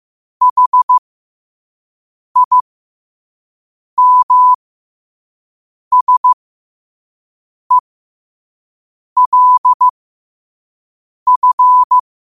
Morse code, heard as a keyed tone, about 15 wpm, spaced as for 5 wpm, 1000 Hz. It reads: HIMSELF
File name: qotdfiles / 5wpm